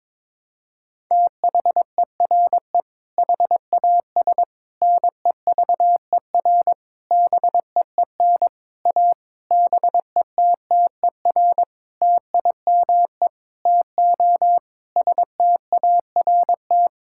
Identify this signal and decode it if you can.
Morse code, heard as a keyed tone, about 21 wpm, 700 Hz. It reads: THERE HAS NEVER BEEN A BETTER TIME TO START
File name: Sentences with 7-Word Phrases - 22wpm